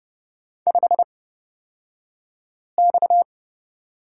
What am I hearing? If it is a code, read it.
Morse code, heard as a keyed tone, about 31 wpm, spaced as for 5 wpm, 700 Hz. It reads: 5X
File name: Lesson 37d - Letter x - Two Characters 30wpm